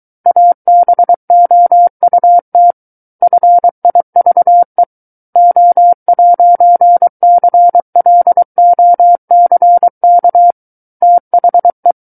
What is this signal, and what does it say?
Morse code, heard as a keyed tone, about 23 wpm, 700 Hz. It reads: ABOUT FIVE O'CLOCK THE
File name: Book One / 14-Chapter 14 - 23 WPM